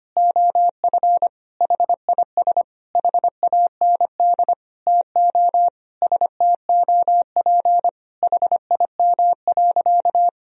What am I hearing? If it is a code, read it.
Morse code, heard as a keyed tone, about 25 wpm, 700 Hz. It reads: OF HIS HAND TO STOP HIM.
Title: Sentences from Top 1000 Words - Review Part 4 - 25wpm